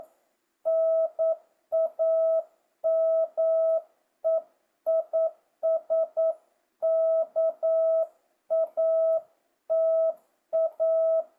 Morse code, heard as a keyed tone, about 9 wpm, 650 Hz. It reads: NAME IS KATA
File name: jh8cbh / 2025cw